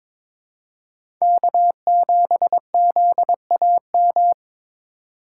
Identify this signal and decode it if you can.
Morse code, heard as a keyed tone, about 22 wpm, 700 Hz. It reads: K7ZAM
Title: USA Call Signs - 5 Characters - 3x Character Spacing - 22wpm